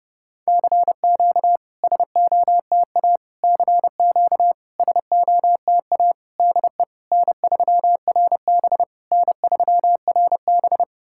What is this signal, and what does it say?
Morse code, heard as a keyed tone, about 30 wpm, 700 Hz. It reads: CQ SOTA CQ SOTA DE N3RB N3RB
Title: SOTA QSO Practice 30wpm